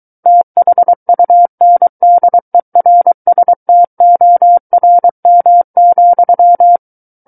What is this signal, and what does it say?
Morse code, heard as a keyed tone, about 23 wpm, 700 Hz. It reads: THUNDERSTORM,
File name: Book One / 14-Chapter 14 - 23 WPM